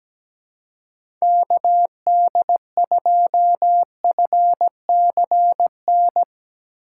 Morse code, heard as a keyed tone, about 17 wpm, 700 Hz. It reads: KD2FCN
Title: USA Call Signs - 6x Character Spacing - 17wpm